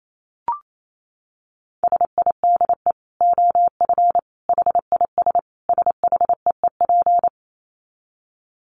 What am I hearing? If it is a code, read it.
Morse code, heard as a keyed tone, about 28 wpm, 700 Hz. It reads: SIDE OF HIS SHEEP
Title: Dit Challenge 28wpm